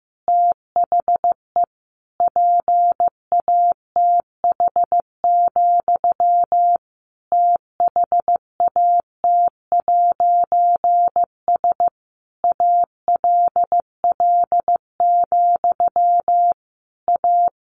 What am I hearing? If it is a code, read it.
Morse code, heard as a keyed tone, about 15 wpm, 700 Hz. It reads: THE PATH, THAT'S ALL, A
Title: A Sound Of Thunder 15wpm